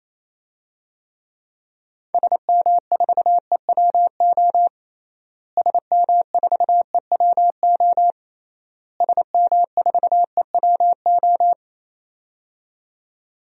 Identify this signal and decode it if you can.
Morse code, heard as a keyed tone, about 27 wpm, 700 Hz. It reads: SM4EWO SM4EWO SM4EWO
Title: DXCC-100 Call Signs - 6 Characters - Repeated 3x - 28wpm